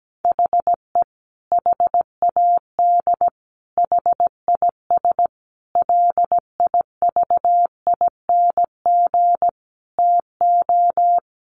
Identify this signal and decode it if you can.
Morse code, heard as a keyed tone, about 17 wpm, 700 Hz. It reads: HE HAD HIS LIVING TO
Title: Sentences from Top 1500 Words 17wpm